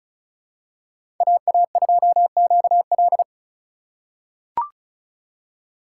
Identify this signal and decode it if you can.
Morse code, heard as a keyed tone, about 34 wpm, 700 Hz. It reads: AA2QL
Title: USA Call Signs - 5 Characters - 3x Character Spacing - 35wpm